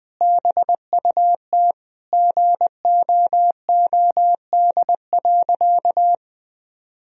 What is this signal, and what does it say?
Morse code, heard as a keyed tone, about 20 wpm, 700 Hz. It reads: BUT GOOD.
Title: Sentences from Top 1000 Words - Review Part 2 - 20wpm